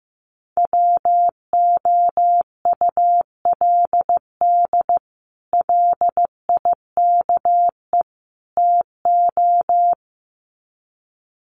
Morse code, heard as a keyed tone, about 15 wpm, 700 Hz. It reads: WOULD LIKE TO
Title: Sets of 3 Words 15wpm